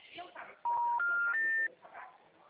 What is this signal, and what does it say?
Special information tone, with speech.